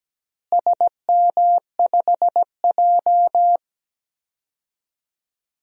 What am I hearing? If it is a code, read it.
Morse code, heard as a keyed tone, about 17 wpm, 700 Hz. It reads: SM5J
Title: DXCC-100 Call Signs - 4 Characters - Repeated 3x - 17wpm